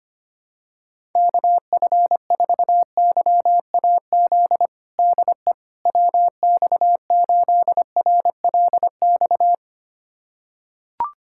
Morse code, heard as a keyed tone, about 25 wpm, 700 Hz. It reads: KF4YAZ DE WX8RLX
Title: USA Callsign DE Callsign 25wpm